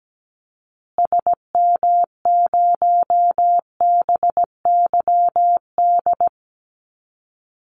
Morse code, heard as a keyed tone, about 17 wpm, 700 Hz. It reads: SM0BYD
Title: Call Signs Repeated 2x - 17wpm